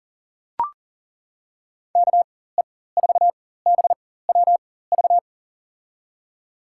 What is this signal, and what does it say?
Morse code, heard as a keyed tone, about 38 wpm, spaced as for 18 wpm, 700 Hz. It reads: KE4BWV
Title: USA Call Signs - 6 Characters - 4x Character Spacing - 40wpm